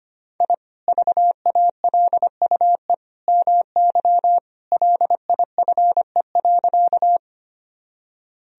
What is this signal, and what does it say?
Morse code, heard as a keyed tone, about 25 wpm, 700 Hz. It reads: I VALUE MY LIFE.
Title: Sentences from Top 1000 Words - Review Part 3 - 25wpm